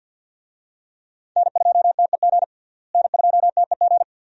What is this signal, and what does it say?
Morse code, heard as a keyed tone, about 51 wpm, 700 Hz. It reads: N2TEG N2TEG
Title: USA Call Signs - 5 Characters - Repeated 2x - 50wpm